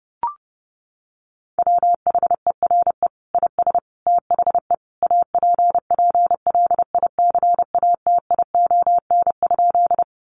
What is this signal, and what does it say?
Morse code, heard as a keyed tone, about 30 wpm, 700 Hz. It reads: WHERE IS THE APPLICATION?
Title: Sentences from Top 1200 Words 30wpm